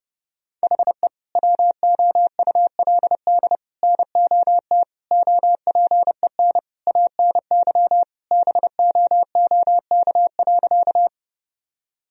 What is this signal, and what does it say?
Morse code, heard as a keyed tone, about 30 wpm, 700 Hz. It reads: HE WOULD NOT OPEN ANY BOOK.
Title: Sentences from Top 1000 Words - Review Part 1 - 30wpm